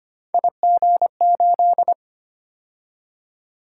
Morse code, heard as a keyed tone, about 25 wpm, 700 Hz. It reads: IG8
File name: Three Letter-Number Combinations - 25wpm